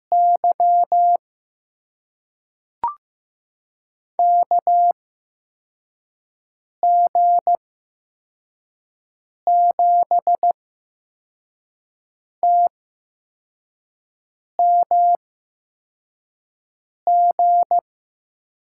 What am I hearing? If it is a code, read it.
Morse code, heard as a keyed tone, about 15 wpm, spaced as for 4 wpm, 700 Hz. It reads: Y KG7TMG
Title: USA Call Signs - 8x Character Spacing - 15wpm